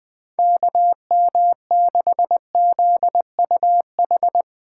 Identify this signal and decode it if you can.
Morse code, heard as a keyed tone, about 20 wpm, 700 Hz. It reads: KM6ZUH
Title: USA Call Signs - 6 Characters - 8x Character Spacing - 20wpm